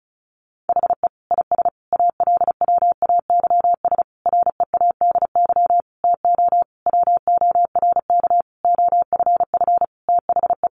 Morse code, heard as a keyed tone, about 34 wpm, 700 Hz. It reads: HE IS ALWAYS READY TO WORK OFF THE
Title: Sentences from Top 1300 Words 35wpm